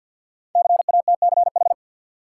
Morse code, heard as a keyed tone, about 51 wpm, 700 Hz. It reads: CATCH
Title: Sentences from Top 1500 Words - New Word Drill Alt - 50wpm